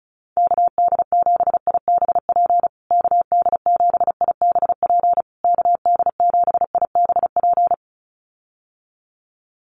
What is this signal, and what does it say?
Morse code, heard as a keyed tone, about 34 wpm, 700 Hz. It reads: KD7IBP KD7IBP KD7IBP
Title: USA Call Signs Repeated 3x 35wpm